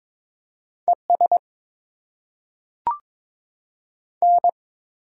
Morse code, heard as a keyed tone, about 22 wpm, 700 Hz. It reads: E S N
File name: Top 20 QSO Elements - 8x Character Spacing - 22wpm